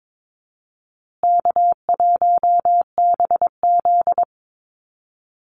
Morse code, heard as a keyed tone, about 22 wpm, 700 Hz. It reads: K1BZ
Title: Easy Call Signs Repeated 3x - 22wpm